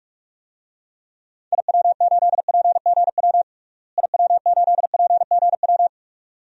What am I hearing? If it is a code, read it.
Morse code, heard as a keyed tone, about 45 wpm, 700 Hz. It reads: IW8PGW IW8PGW
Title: DXCC-100 Call Signs - 6 Characters - Repeated 2x - 45wpm